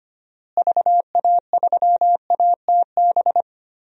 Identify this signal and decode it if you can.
Morse code, heard as a keyed tone, about 25 wpm, 700 Hz. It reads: VA3ATB
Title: DXCC-100 Call Signs - 6 Characters - Repeated 3x - 25wpm